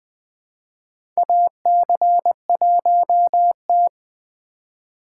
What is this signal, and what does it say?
Morse code, heard as a keyed tone, about 20 wpm, 700 Hz. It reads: AC1T
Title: USA Call Signs - 4 Characters - 4x Character Spacing - 20wpm